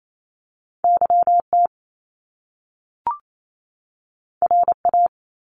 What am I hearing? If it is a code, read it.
Morse code, heard as a keyed tone, about 28 wpm, 700 Hz. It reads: YT RA
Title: Top Two Letter Combinations in Words 28wpm